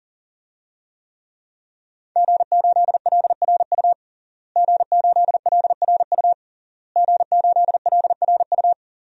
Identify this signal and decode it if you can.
Morse code, heard as a keyed tone, about 40 wpm, 700 Hz. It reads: G8LRU G8LRU G8LRU
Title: DXCC-100 Call Signs - 5 Characters - Repeated 3x - 40wpm